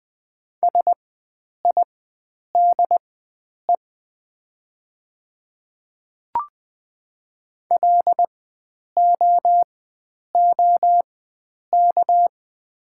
Morse code, heard as a keyed tone, about 20 wpm, spaced as for 9 wpm, 690 Hz. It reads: SIDE LOOK
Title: 4x Character Spacing - Top 100 Words - 20wpm